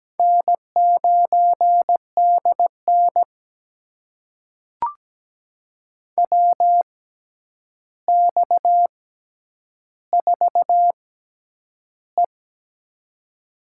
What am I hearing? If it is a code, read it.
Morse code, heard as a keyed tone, about 17 wpm, spaced as for 9 wpm, 700 Hz. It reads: N9DN WX4E
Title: USA Call Signs - 4 Characters - 6x Character Spacing - 17wpm